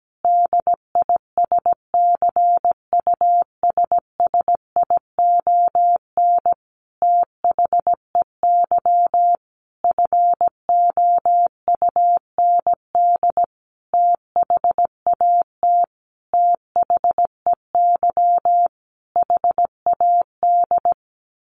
Morse code, heard as a keyed tone, about 17 wpm, 700 Hz. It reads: DISCUSSION THEY FOUND THAT THEY HAD
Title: Sentences from Top 1200 Words 17wpm